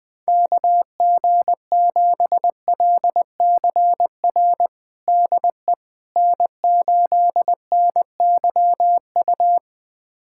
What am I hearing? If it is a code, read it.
Morse code, heard as a keyed tone, about 20 wpm, 700 Hz. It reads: KG7LCR DE N8NYU
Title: USA Callsign DE Callsign 20wpm